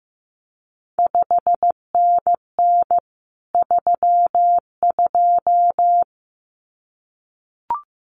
Morse code, heard as a keyed tone, about 16 wpm, 700 Hz. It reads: 5NN 32
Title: CQ WW DX Contest Exchange - 15wpm